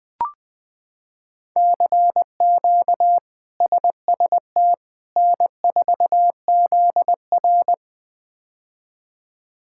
Morse code, heard as a keyed tone, about 20 wpm, 700 Hz. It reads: CQ SST N4ZR
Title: K1USN SST QSO Practice - 20wpm